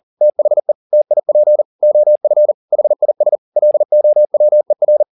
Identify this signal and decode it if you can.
Morse code, heard as a keyed tone, about 40 wpm, 600 Hz. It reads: THE TIP OF HIS LOWER
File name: the old man and the sea 2 40wpm